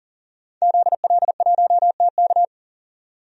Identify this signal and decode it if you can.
Morse code, heard as a keyed tone, about 40 wpm, 700 Hz. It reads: ZL1TK
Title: DXCC-100 Call Signs - 5 Characters - 40wpm